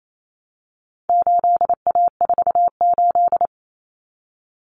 Morse code, heard as a keyed tone, about 28 wpm, 700 Hz. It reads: 8A48
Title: Common Cut Numbers - 4-Digit 28wpm